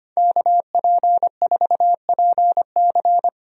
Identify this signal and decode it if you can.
Morse code, heard as a keyed tone, about 25 wpm, 700 Hz. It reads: KP4PC
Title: DXCC-100 Call Signs - 5 Characters - Repeated 3x - 25wpm